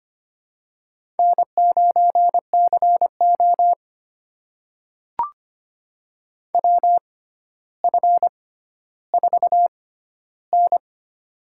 Morse code, heard as a keyed tone, about 25 wpm, spaced as for 13 wpm, 700 Hz. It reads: N9CO WF4N
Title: USA Call Signs - 4 Characters - 6x Character Spacing - 25wpm